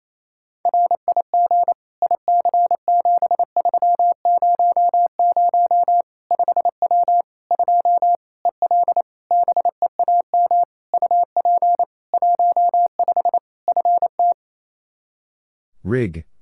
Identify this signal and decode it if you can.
Morse code, heard as a keyed tone, about 28 wpm, 700 Hz. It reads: RIG IC7300 5W 2 EL BEAM UP 15 FT
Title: K1USN SST QSO Practice 28wpm